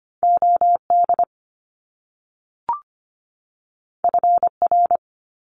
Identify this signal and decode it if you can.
Morse code, heard as a keyed tone, about 25 wpm, 700 Hz. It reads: OD FR
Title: Top Two Letter Combinations in Words 25wpm